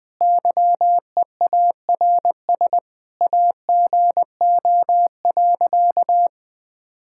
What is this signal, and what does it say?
Morse code, heard as a keyed tone, about 20 wpm, 700 Hz. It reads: YEARS AGO.
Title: Sentences with 2-Word Phrases - 20wpm